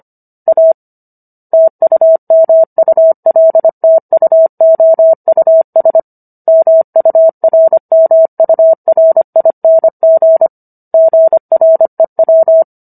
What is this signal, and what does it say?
Morse code, heard as a keyed tone, about 25 wpm, 650 Hz. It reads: A TUMULTUOUS MURMURING GREW
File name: Book One / 16-Chapter 16 - 25 WPM